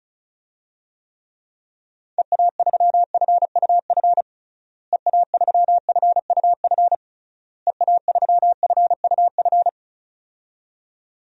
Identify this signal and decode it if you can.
Morse code, heard as a keyed tone, about 34 wpm, 700 Hz. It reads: EA3FUF EA3FUF EA3FUF
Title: DXCC-100 Call Signs - 6 Characters - Repeated 3x - 35wpm